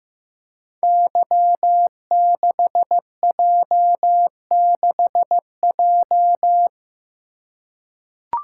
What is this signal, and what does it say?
Morse code, heard as a keyed tone, about 15 wpm, 700 Hz. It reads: Y6J6J
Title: Code Groups - 5 Letter:Numbers - 15wpm